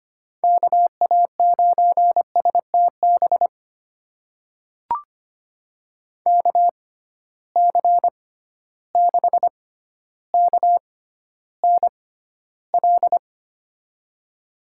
Morse code, heard as a keyed tone, about 25 wpm, spaced as for 13 wpm, 700 Hz. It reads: KA9STB KC6KNL
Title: USA Call Signs - 6 Characters - 6x Character Spacing - 25wpm